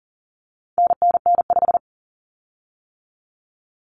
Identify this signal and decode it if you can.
Morse code, heard as a keyed tone, about 40 wpm, 700 Hz. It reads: NNN5
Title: Common Cut Numbers - 4-Digit 40wpm